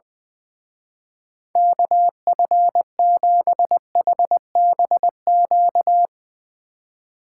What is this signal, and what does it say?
Morse code, heard as a keyed tone, about 20 wpm, 700 Hz. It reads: KF7HBQ
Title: Speed Racing - USA 2x3 Call Signs 20wpm